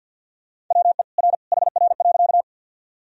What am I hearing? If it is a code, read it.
Morse code, heard as a keyed tone, about 48 wpm, 700 Hz. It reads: WE R HR.
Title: Sentences from Top 200 Words and Abbreviations 50wpm